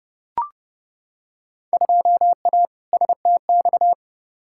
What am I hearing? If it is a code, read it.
Morse code, heard as a keyed tone, about 32 wpm, 700 Hz. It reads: 2A STX
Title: Field Day Exchanges - 30wpm